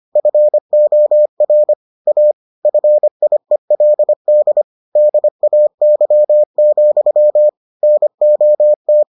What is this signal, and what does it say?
Morse code, heard as a keyed tone, about 25 wpm, 600 Hz. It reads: FOR A FIELD DAY, NOT